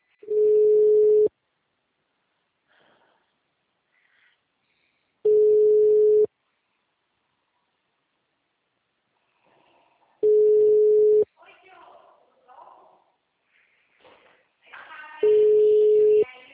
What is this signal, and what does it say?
Ringback tone (Europe, 425 Hz), with speech.
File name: out- -2011- - - .45037